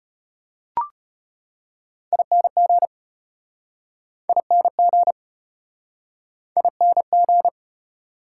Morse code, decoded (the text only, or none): ING ING ING